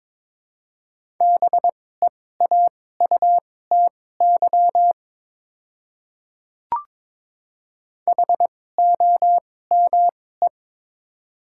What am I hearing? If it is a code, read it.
Morse code, heard as a keyed tone, about 22 wpm, spaced as for 16 wpm, 700 Hz. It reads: BEAUTY HOME
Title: 2x Character Spacing - Top 500 Words - 22wpm